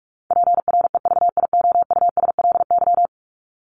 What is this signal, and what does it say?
Morse code, heard as a keyed tone, about 44 wpm, 700 Hz. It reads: PREVIOUSLY
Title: LY Suffix - Top 100 Words with Base Words 45wpm